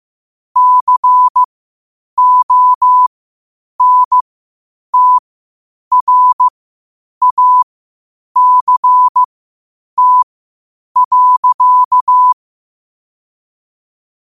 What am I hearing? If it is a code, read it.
Morse code, heard as a keyed tone, about 15 wpm, spaced as for 8 wpm, 1000 Hz. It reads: CONTRACT.
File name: qotdfiles / 8wpm